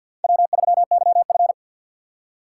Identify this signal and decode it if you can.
Morse code, heard as a keyed tone, about 47 wpm, 700 Hz. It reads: W3YF
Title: USA Call Signs - 4 Characters - 2x Character Spacing - 50wpm